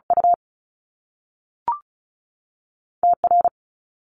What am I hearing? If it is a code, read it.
Morse code, heard as a keyed tone, about 36 wpm, 700 Hz. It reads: U T R
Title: Top Two Letter Combinations in Words 35wpm